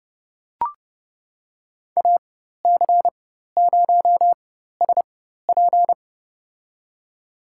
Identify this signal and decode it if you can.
Morse code, heard as a keyed tone, about 32 wpm, spaced as for 14 wpm, 700 Hz. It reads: AC0SP
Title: USA Call Signs - 4x Character Spacing - 30wpm